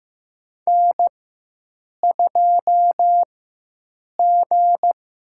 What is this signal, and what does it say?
Morse code, heard as a keyed tone, about 15 wpm, 700 Hz. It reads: N 2 G
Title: USA Call Signs - 4 Characters - 4x Character Spacing - 15wpm